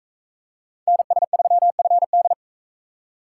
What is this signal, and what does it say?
Morse code, heard as a keyed tone, about 39 wpm, 700 Hz. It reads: NS3FD